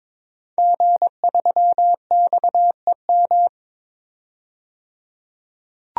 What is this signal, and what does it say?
Morse code, heard as a keyed tone, about 22 wpm, 700 Hz. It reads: G3XEM
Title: DXCC-100 Call Signs - 5 Characters - 22wpm